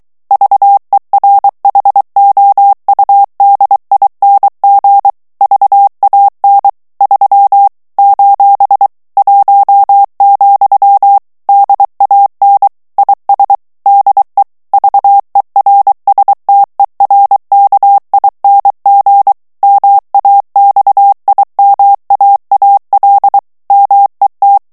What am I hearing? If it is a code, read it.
Morse code, heard as a keyed tone, about 23 wpm, 800 Hz. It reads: VERHOUDING VAN 3 : 1, DAN IS DE VERSTERKING MAXIMAAL MET
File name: HOLLANDAIS texte clair 1 vitesse 22 mots minute